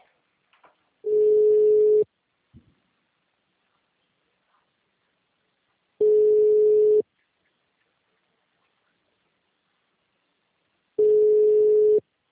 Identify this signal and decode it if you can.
Ringback tone (Europe, 425 Hz).
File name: out- -2010